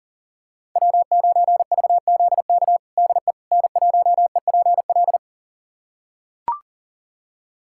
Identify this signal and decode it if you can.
Morse code, heard as a keyed tone, about 40 wpm, 700 Hz. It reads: W9VZK DE N1EPL
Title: USA Callsign DE Callsign - 5 Characters - 40wpm